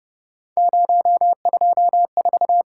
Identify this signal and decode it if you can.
Morse code, heard as a keyed tone, about 30 wpm, 700 Hz. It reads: 024